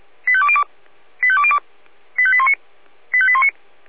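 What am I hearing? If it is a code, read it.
Five-tone selective call (ZVEI), heard as quick runs of five tones, tones of about 70 ms. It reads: 85202 85202 85819 85819
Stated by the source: ZVEI five-tone selective call received by SDR (496).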